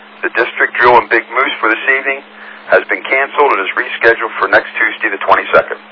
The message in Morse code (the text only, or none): none